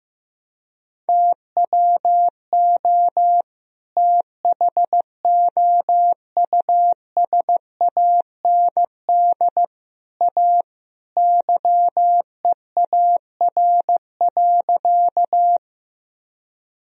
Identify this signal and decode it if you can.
Morse code, heard as a keyed tone, about 15 wpm, 700 Hz. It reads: TWO THOUSAND A YEAR.
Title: Sentences from Top 1000 Words - Review Part 3 - 15wpm